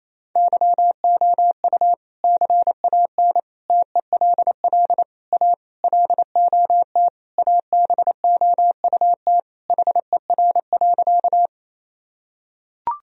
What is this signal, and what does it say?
Morse code, heard as a keyed tone, about 28 wpm, 700 Hz. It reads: YOU CAN TELL A LOT ABOUT HER.
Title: Sentences from Top 1000 Words - Review Part 3 - 28wpm